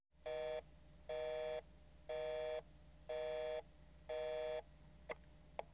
Busy tone (North America), 480 + 620 Hz.